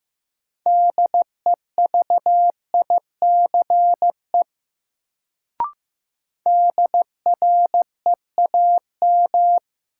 Morse code, heard as a keyed tone, about 15 wpm, 700 Hz. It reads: DEVICE DREAM
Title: Sentences from Top 1500 Words - New Word Drill - 15wpm